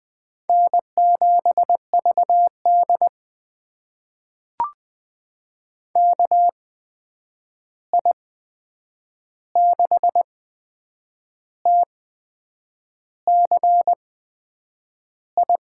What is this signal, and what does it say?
Morse code, heard as a keyed tone, about 20 wpm, spaced as for 8 wpm, 700 Hz. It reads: N7VD KI6TCI
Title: USA Call Signs - 8x Character Spacing - 20wpm